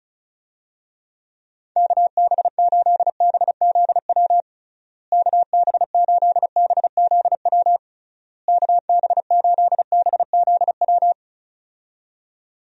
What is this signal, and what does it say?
Morse code, heard as a keyed tone, about 36 wpm, 700 Hz. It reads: KB8BZW KB8BZW KB8BZW
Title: USA Call Signs - 6 Characters - Repeated 3x - 35wpm